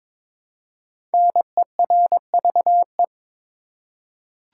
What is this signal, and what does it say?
Morse code, heard as a keyed tone, about 22 wpm, 700 Hz. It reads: NERVE
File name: 5 Letter Words - First 3 Letters Given - 22wpm